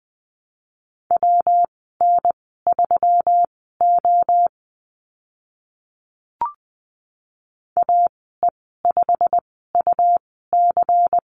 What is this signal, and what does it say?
Morse code, heard as a keyed tone, about 20 wpm, spaced as for 15 wpm, 700 Hz. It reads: WN3O AE5UC
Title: 147 Lesson 24g - Letter y - Call Signs 20wpm